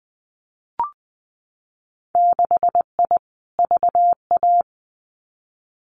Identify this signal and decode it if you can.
Morse code, heard as a keyed tone, about 20 wpm, 700 Hz. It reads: 6I VA
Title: Winter Field Day Exchanges Encore 20wpm